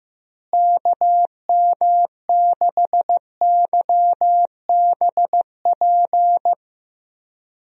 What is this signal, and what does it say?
Morse code, heard as a keyed tone, about 15 wpm, 700 Hz. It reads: KM6YBP
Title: USA Call Signs - 6 Characters - Repeated 3x - 15wpm